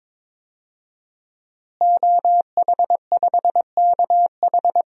Morse code, heard as a keyed tone, about 22 wpm, 700 Hz. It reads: OH5KH
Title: DXCC-100 Call Signs - 5 Characters - Repeated 2x - 22wpm